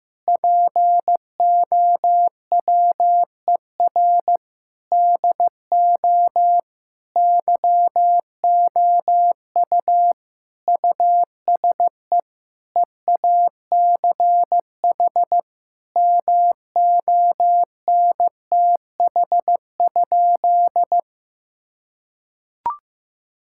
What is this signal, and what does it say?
Morse code, heard as a keyed tone, about 15 wpm, 700 Hz. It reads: POWER DO YOU USE EACH MONTH?
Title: Sentences from Top 1000 Words - Review Part 6 - 15wpm